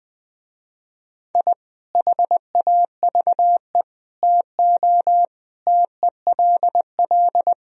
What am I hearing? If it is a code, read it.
Morse code, heard as a keyed tone, about 20 wpm, 700 Hz. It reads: I HAVE TO TELL